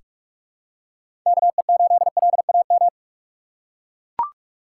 Morse code, heard as a keyed tone, about 44 wpm, 700 Hz. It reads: KE8LAM
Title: USA Call Signs - 3x Character Spacing - 45wpm